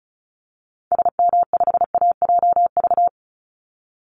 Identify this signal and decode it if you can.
Morse code, heard as a keyed tone, about 34 wpm, 700 Hz. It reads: SM5AJV
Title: Call Signs Repeated 3x - 35wpm